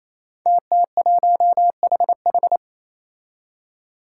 Morse code, heard as a keyed tone, about 27 wpm, 700 Hz. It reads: TT1HH
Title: Five Challenging Letter-Number Combinations 28wpm